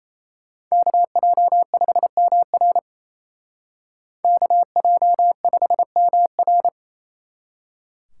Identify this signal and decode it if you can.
Morse code, heard as a keyed tone, about 34 wpm, 700 Hz. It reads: KJ5MR KJ5MR
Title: Speed Racing - USA 2x2 Call Signs 28wpm